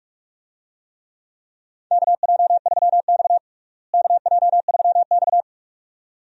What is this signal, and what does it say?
Morse code, heard as a keyed tone, about 46 wpm, 700 Hz. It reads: KJ3X KJ3X
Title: USA Call Signs - 4 Characters - Repeated 2x - 45wpm